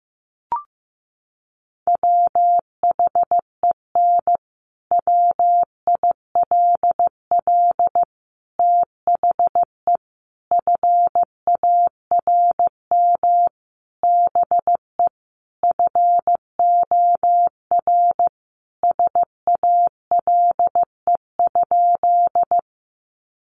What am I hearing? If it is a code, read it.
Morse code, heard as a keyed tone, about 14 wpm, 700 Hz. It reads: WHEN WILL THE FARM BE FOR SALE?
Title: Sentences from Top 1400 Words 15wpm